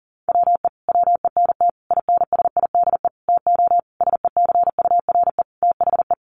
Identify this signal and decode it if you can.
Morse code, heard as a keyed tone, about 40 wpm, 700 Hz. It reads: WE WENT INSIDE TO SECURE THE
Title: Sentences from Top 1200 Words 40wpm